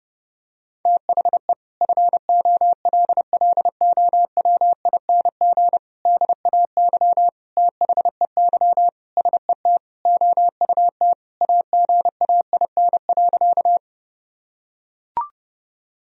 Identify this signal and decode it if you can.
Morse code, heard as a keyed tone, about 30 wpm, 700 Hz. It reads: THE FOLLOWING DAY THEY SET OUT AGAIN.
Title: Sentences from Top 1000 Words - Review Part 4 - 30wpm